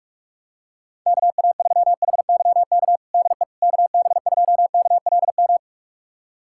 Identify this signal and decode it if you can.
Morse code, heard as a keyed tone, about 46 wpm, 700 Hz. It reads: KA3HYK DE KB2KLM
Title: USA Callsign DE Callsign - 6 Characters - 45wpm